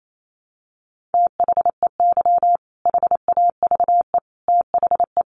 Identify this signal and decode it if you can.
Morse code, heard as a keyed tone, about 28 wpm, 700 Hz. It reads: THEY HAVE THE
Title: Sentences from Top 1300 Words 28wpm